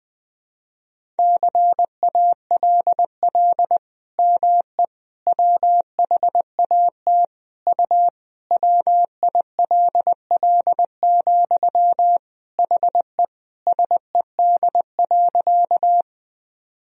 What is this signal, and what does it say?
Morse code, heard as a keyed tone, about 20 wpm, 700 Hz. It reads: CALL ME WHAT U WILL, HE SED.
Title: Sentences from Top 200 Words and Abbreviations 20wpm